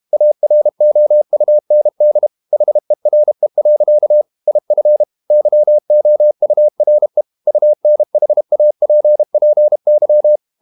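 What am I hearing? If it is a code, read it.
Morse code, heard as a keyed tone, about 32 wpm, 600 Hz. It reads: AROUND HERE. IF YOURE UNHAPPY
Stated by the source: Morse code practice at 32 wpm from CW med Gnister, reading Ken Follett's novel Never in English.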